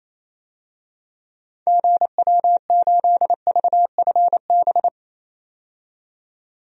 Morse code, heard as a keyed tone, about 28 wpm, 700 Hz. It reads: GW8VFB
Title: DXCC-100 Call Signs - 6 Characters - 28wpm